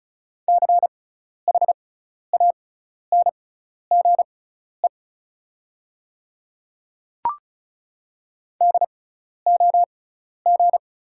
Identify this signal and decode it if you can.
Morse code, heard as a keyed tone, about 36 wpm, spaced as for 12 wpm, 690 Hz. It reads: CHANGE DOG
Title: 6x Character Spacing - Top 500 Words - 35wpm